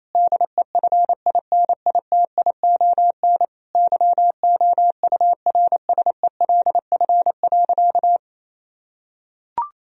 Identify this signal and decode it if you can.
Morse code, heard as a keyed tone, about 27 wpm, 700 Hz. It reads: DEFINITION YOURSELF.
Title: Sentences from Top 1000 Words - Review Part 4 - 28wpm